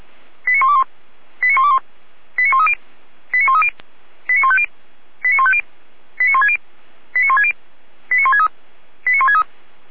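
Five-tone selective call (ZVEI), heard as quick runs of five tones, tones of about 70 ms. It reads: 89121 89121 89130 89130 89150 89150 89160 89160 89163 89163
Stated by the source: ZVEI five-tone call received by SDR, recorded by Feuerwehr Orlamünde (f85360).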